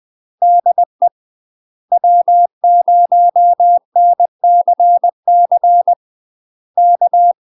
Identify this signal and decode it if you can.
Morse code, heard as a keyed tone, about 20 wpm, 700 Hz. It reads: DE W0NCC K
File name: UK MASTER QSOs TRAINING / 20WPM / UK MASTER QSOs 0024